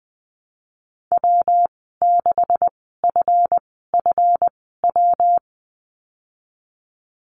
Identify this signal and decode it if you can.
Morse code, heard as a keyed tone, about 20 wpm, spaced as for 15 wpm, 700 Hz. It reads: W6FFW
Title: 161 Lesson 26g - Letter g - Call Signs 20wpm